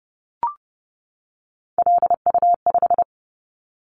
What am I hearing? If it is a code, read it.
Morse code, heard as a keyed tone, about 30 wpm, 700 Hz. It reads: LU5
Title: Callsign Prefixes for Top 100 DXCC - 30wpm